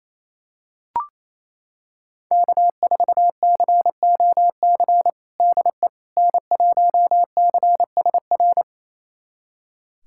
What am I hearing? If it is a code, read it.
Morse code, heard as a keyed tone, about 28 wpm, 700 Hz. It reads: K4COC DE N1CSR
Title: USA Callsign DE Callsign 28wpm